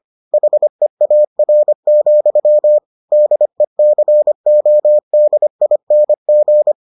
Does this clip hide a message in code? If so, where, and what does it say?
Yes: Morse code from 0.3 to 6.7 s, heard as a keyed tone, about 25 wpm, 600 Hz. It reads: HEAR, DECODING